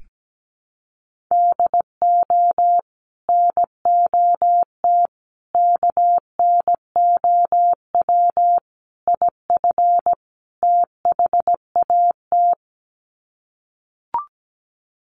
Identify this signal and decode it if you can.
Morse code, heard as a keyed tone, about 17 wpm, 700 Hz. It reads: DO NOT KNOW IF THAT
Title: Sets of 5 Words Encore2 17wpm